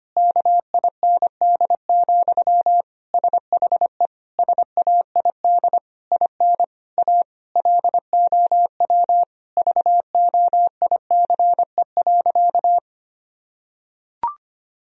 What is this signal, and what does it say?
Morse code, heard as a keyed tone, about 25 wpm, 700 Hz. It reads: KIND, SHE SAID IN A LOW VOICE.
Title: Sentences from Top 1000 Words - Review Part 1 - 25wpm